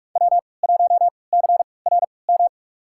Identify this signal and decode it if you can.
Morse code, heard as a keyed tone, about 47 wpm, spaced as for 26 wpm, 700 Hz. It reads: W1CRM
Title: USA Call Signs - 3x Character Spacing - 45wpm